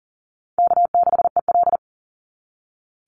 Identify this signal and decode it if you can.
Morse code, heard as a keyed tone, about 42 wpm, 700 Hz. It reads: K6EL
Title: NA SOTA Chaser Callsigns 40wpm